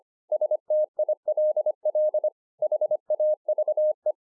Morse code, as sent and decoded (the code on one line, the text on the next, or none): ... - .. .-.. .-.. / .... .- ...- .
STILL HAVE